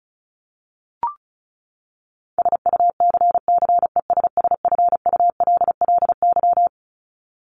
Morse code, heard as a keyed tone, about 34 wpm, 700 Hz. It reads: SUCCESSFULLY
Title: Top 1501-1750 Words 35wpm